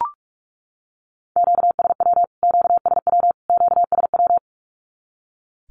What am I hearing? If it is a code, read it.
Morse code, heard as a keyed tone, about 44 wpm, 700 Hz. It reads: QSW QSW QSW
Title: Q-Codes Repeated 3x 45wpm